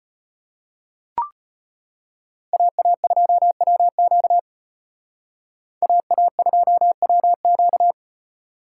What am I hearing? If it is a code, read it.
Morse code, heard as a keyed tone, about 32 wpm, 700 Hz. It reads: AA2WQ AA2WQ